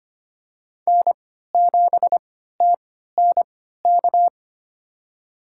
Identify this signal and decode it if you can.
Morse code, heard as a keyed tone, about 25 wpm, spaced as for 14 wpm, 700 Hz. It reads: N7TNK
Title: USA Call Signs - 5 Characters - 3x Character Spacing - 25wpm